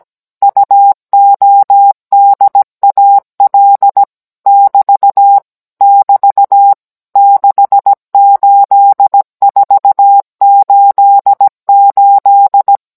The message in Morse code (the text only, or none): UODAL <BT> <BT> 68488